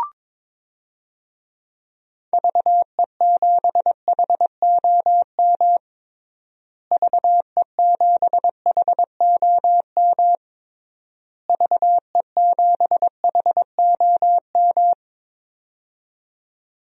Morse code, heard as a keyed tone, about 22 wpm, 700 Hz. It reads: VE7HOM VE7HOM VE7HOM
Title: DXCC-100 Call Signs - 6 Characters - Repeated 3x - 22wpm